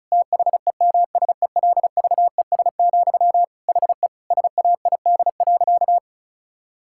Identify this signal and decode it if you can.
Morse code, heard as a keyed tone, about 36 wpm, 700 Hz. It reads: THEMSELVES, HE SAID.
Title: Sentences from Top 1000 Words - Review Part 6 - 35wpm